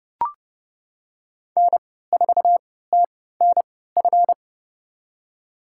Morse code, heard as a keyed tone, about 32 wpm, spaced as for 17 wpm, 700 Hz. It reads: N4TNF
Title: USA Call Signs - 5 Characters - 3x Character Spacing - 30wpm